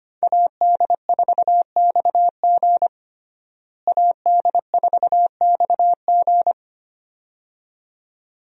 Morse code, heard as a keyed tone, about 25 wpm, 700 Hz. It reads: AD4XG AD4XG
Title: USA Call Signs - 5 Characters - Repeated 2x - 25wpm